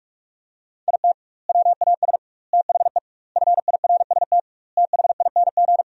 Morse code, heard as a keyed tone, about 46 wpm, 700 Hz. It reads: IT WAS THE FIRST THING
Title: 2x Word-Spacing - Sentences from Top 300 Words - 45wpm